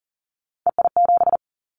Morse code, heard as a keyed tone, about 38 wpm, 700 Hz. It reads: EI7
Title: Callsign Prefixes for Top 50 DXCC - 40wpm